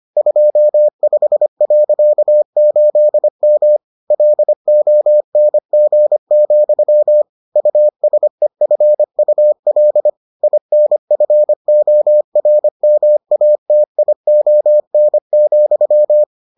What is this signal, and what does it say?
Morse code, heard as a keyed tone, about 25 wpm, 600 Hz. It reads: 25.8M LONG, USEFUL INFORMATION,